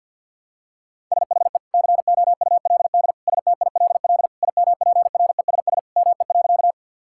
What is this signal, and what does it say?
Morse code, heard as a keyed tone, about 50 wpm, 700 Hz. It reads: SHE COULD STILL IMPRESS ME.
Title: Sentences from Top 1600 Words 50wpm